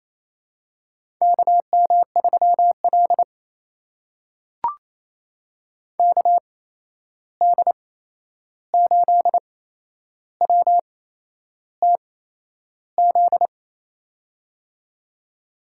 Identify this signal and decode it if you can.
Morse code, heard as a keyed tone, about 28 wpm, spaced as for 11 wpm, 700 Hz. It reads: KM3L KD8WTZ
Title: USA Call Signs - 8x Character Spacing - 28wpm